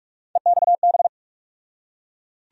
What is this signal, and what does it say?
Morse code, heard as a keyed tone, about 42 wpm, 700 Hz. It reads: EXB
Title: USA Call Sign Postfixes - 3 Letters 45wpm